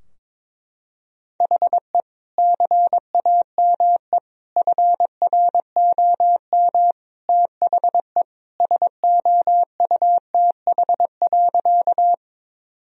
Morse code, heard as a keyed tone, about 22 wpm, 700 Hz. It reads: HE CAME FROM THE SOUTH.
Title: Sentences from Top 1000 Words - Review Part 2 - 22wpm